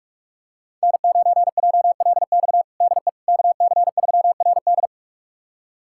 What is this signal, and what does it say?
Morse code, heard as a keyed tone, about 41 wpm, 700 Hz. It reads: N9JLX DE KC3RD
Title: USA Callsign DE Callsign 45wpm